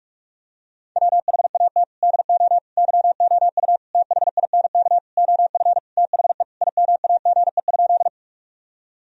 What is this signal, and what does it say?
Morse code, heard as a keyed tone, about 41 wpm, 700 Hz. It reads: WHAT DO YOU THINK OF THE IMAGE?
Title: Sentences from Top 1000 Words - Review Part 4 - 45wpm